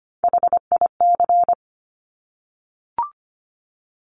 Morse code, heard as a keyed tone, about 25 wpm, 700 Hz. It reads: HIC
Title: Top Three Letter Combinations in Words 25wpm